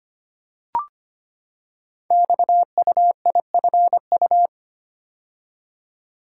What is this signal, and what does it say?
Morse code, heard as a keyed tone, about 25 wpm, 700 Hz. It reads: XUIFU
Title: Code Groups - 5 Letters - 25wpm